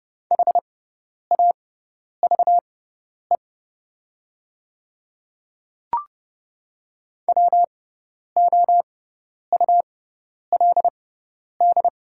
Morse code, heard as a keyed tone, about 30 wpm, spaced as for 10 wpm, 700 Hz. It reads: HAVE WOULD
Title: 6x Character Spacing - Top 100 Words - 30wpm